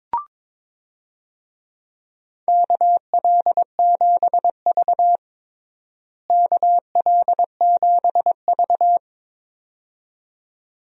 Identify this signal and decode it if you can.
Morse code, heard as a keyed tone, about 22 wpm, 700 Hz. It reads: KL7V KL7V
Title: DXCC-100 Call Signs - 4 Characters - Repeated 2x - 22wpm